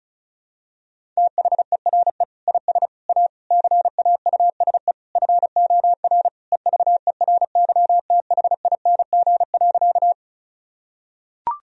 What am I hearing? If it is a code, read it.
Morse code, heard as a keyed tone, about 36 wpm, 700 Hz. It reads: THERE IS A CAUSE FOR EVERYTHING.
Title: Sentences from Top 1000 Words - Review Part 1 - 35wpm